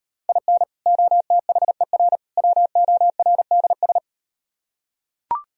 Morse code, heard as a keyed tone, about 38 wpm, 700 Hz. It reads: IN OTHER WORDS